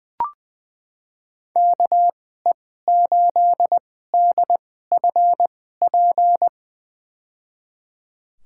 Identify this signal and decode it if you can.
Morse code, heard as a keyed tone, about 21 wpm, spaced as for 15 wpm, 700 Hz. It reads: KE8DFP
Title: USA Call Signs - 6 Characters - 2x Character Spacing - 20wpm